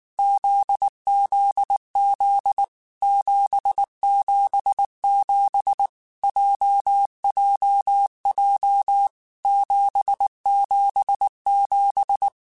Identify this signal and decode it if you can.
Morse code, heard as a keyed tone, about 19 wpm, 800 Hz. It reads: ZZZ 777 JJJ 777
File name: Basic Sets / Basic Letter Groups Collective7 / JZ17@18